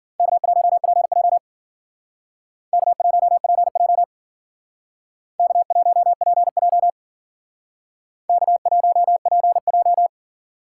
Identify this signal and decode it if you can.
Morse code, heard as a keyed tone, about 52 wpm, 700 Hz. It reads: K1PJ K1PJ K1PJ K1PJ